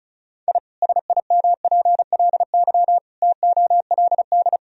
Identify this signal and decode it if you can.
Morse code, heard as a keyed tone, about 36 wpm, 700 Hz. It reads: I SIMPLY TOLD
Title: Sentences from Top 1000 Words - Review Part 4 - 35wpm